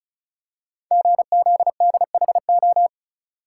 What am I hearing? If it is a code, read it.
Morse code, heard as a keyed tone, about 36 wpm, 700 Hz. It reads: GZDHO
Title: Code Groups - 5 Letters - 35wpm